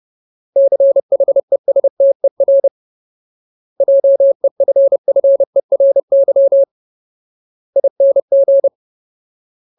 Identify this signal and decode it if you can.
Morse code, heard as a keyed tone, about 30 wpm, 550 Hz. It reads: CHESTER JEFFERY ING
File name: Ultimate Word List / 30WPM Ultimate